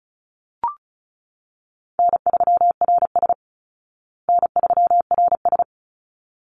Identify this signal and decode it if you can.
Morse code, heard as a keyed tone, about 34 wpm, 700 Hz. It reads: N3RS N3RS
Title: Easy Call Signs Repeated 2x - 35wpm